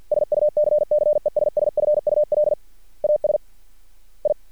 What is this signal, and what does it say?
Morse code, heard as a keyed tone, about 50 wpm, 600 Hz. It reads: SUCCESSFUL AS I
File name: archived / CRSnet